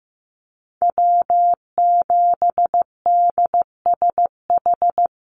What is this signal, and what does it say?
Morse code, heard as a keyed tone, about 15 wpm, 700 Hz. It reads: W7DSH
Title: USA Call Signs Repeated 3x 15wpm